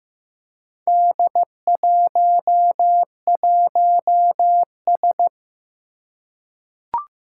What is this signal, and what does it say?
Morse code, heard as a keyed tone, about 15 wpm, 700 Hz. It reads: D11S
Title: Four Challenging Letter-Number Combinations 15wpm